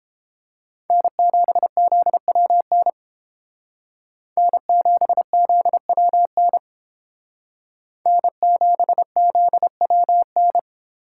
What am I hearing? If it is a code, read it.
Morse code, heard as a keyed tone, about 30 wpm, 700 Hz. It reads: N7ZWN N7ZWN N7ZWN